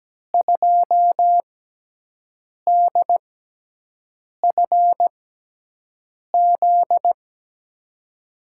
Morse code, heard as a keyed tone, about 17 wpm, spaced as for 6 wpm, 700 Hz. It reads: 2DFZ
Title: USA Call Signs - 6 Characters - 6x Character Spacing - 17wpm